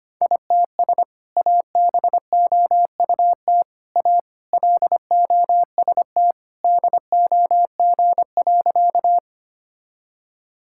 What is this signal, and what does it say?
Morse code, heard as a keyed tone, about 25 wpm, 700 Hz. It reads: ITS ABOUT A LOST DOG.
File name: practice / 25wpm / Sentences with 2-Word Phrases - 25wpm